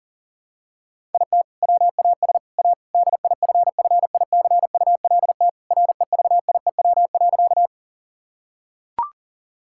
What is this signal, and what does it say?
Morse code, heard as a keyed tone, about 40 wpm, 700 Hz. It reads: IT WAS A DIFFICULT REVIEW.
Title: Sentences from Top 1000 Words - Review Part 6 - 40wpm